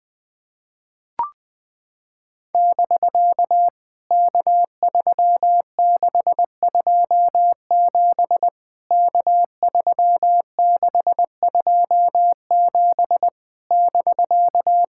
Morse code, heard as a keyed tone, about 21 wpm, 700 Hz. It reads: <BK> K3627 K3627 <BK>
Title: POTA QSO Practice - 20wpm